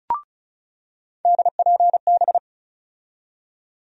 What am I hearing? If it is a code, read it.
Morse code, heard as a keyed tone, about 36 wpm, 700 Hz. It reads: DPB
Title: USA Call Sign Postfixes - 3 Letters 35wpm